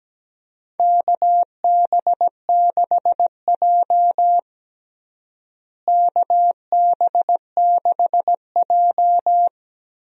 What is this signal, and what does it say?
Morse code, heard as a keyed tone, about 17 wpm, 700 Hz. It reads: KB6J KB6J
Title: USA Call Signs - 4 Characters - Repeated 2x - 17wpm